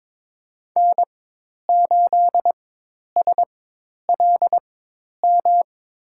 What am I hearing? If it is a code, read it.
Morse code, heard as a keyed tone, about 22 wpm, spaced as for 10 wpm, 700 Hz. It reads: N8SLM
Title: USA Call Signs - 5 Characters - 4x Character Spacing - 22wpm